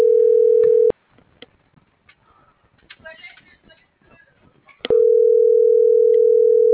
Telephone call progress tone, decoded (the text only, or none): ringback tone